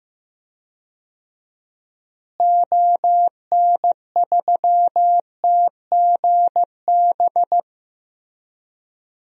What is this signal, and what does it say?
Morse code, heard as a keyed tone, about 15 wpm, 700 Hz. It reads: ON3TGB
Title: DXCC-100 Call Signs - 6 Characters - Repeated 2x - 15wpm